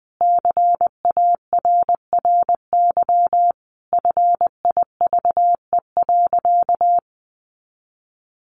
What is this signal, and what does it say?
Morse code, heard as a keyed tone, about 20 wpm, 700 Hz. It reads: CARRY FIVE.
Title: Sentences from Top 400 Words 20wpm